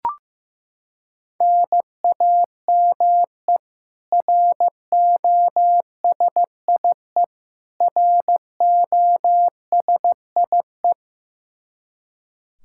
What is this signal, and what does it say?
Morse code, heard as a keyed tone, about 15 wpm, 700 Hz. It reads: NAME ROSIE ROSIE
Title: K1USN SST QSO Practice 15wpm